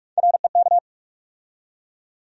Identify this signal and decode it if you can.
Morse code, heard as a keyed tone, about 46 wpm, 700 Hz. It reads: REK
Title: USA Call Sign Postfixes - 3 Letters 45wpm